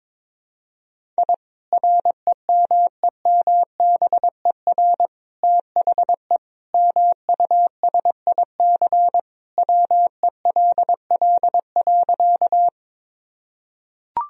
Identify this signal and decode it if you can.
Morse code, heard as a keyed tone, about 22 wpm, 700 Hz. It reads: I REMEMBER THE MUSIC WELL.
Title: Sentences from Top 1000 Words - Review Part 1 - 22wpm